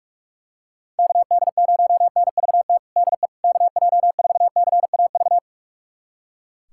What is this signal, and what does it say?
Morse code, heard as a keyed tone, about 43 wpm, 690 Hz. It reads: KD0NVT DE KJ4CAV
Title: USA Callsign DE Callsign 45wpm